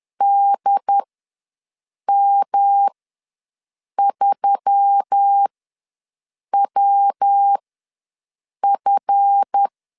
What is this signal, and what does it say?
Morse code, heard as a keyed tone, about 10 wpm, spaced as for 6 wpm, 800 Hz. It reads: DM3WF
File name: LEZIONI / 7 GF9Z / 7 GF ws 10wpm 3918